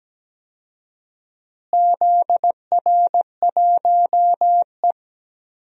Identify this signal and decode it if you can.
Morse code, heard as a keyed tone, about 17 wpm, 700 Hz. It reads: ZR1E
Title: DXCC-100 Call Signs - 4 Characters - Repeated 2x - 17wpm